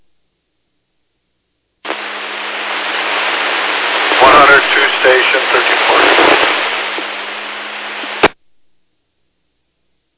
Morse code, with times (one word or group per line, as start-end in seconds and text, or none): none